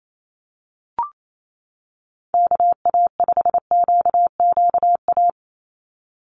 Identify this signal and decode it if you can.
Morse code, heard as a keyed tone, about 28 wpm, 700 Hz. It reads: KA5QQA
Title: NA SOTA Chaser Callsigns 28wpm